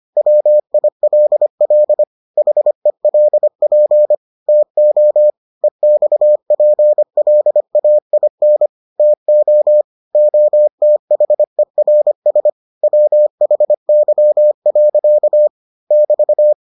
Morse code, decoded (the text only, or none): WILL HELP TO EXPLAIN TO OTHERS WHY. <BT>